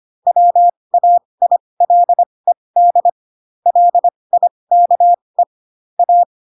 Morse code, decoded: WAILED LIKE A

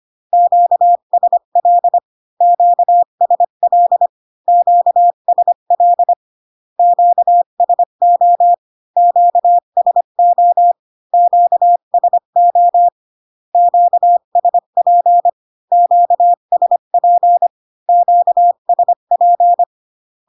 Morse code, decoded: QSL QSL QSL QSO QSO QSO QSP QSP QSP

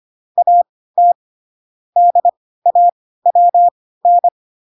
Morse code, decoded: AT DAWN

